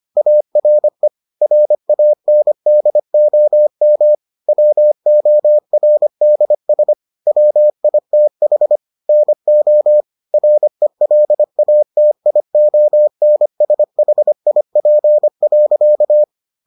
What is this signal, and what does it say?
Morse code, heard as a keyed tone, about 25 wpm, 600 Hz. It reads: ARE RANDOM WORDS WITH NO RELATIONSHIP.